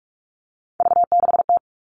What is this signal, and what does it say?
Morse code, heard as a keyed tone, about 44 wpm, 700 Hz. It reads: V6T
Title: All Cut Numbers - 3-Digit 45wpm